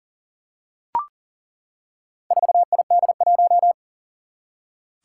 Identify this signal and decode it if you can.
Morse code, heard as a keyed tone, about 39 wpm, 700 Hz. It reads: 4ID1
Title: Four Challenging Letter-Number Combinations 40wpm